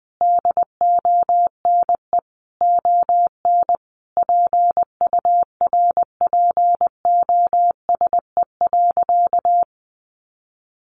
Morse code, heard as a keyed tone, about 20 wpm, 700 Hz. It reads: DONE ON PURPOSE.
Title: Sentences from Top 800 Words 20wpm